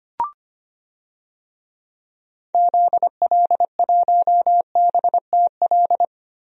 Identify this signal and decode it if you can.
Morse code, heard as a keyed tone, about 25 wpm, 700 Hz. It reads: ZL1BTL
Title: DXCC-100 Call Signs - 6 Characters - Repeated 3x - 25wpm